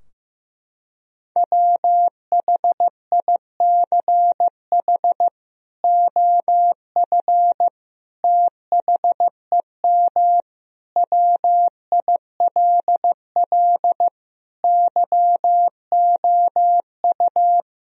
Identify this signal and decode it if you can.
Morse code, heard as a keyed tone, about 15 wpm, 700 Hz. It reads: WHICH OF THEM WILL YOU
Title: Sentences from Top 1000 Words - Review Part 3 - 15wpm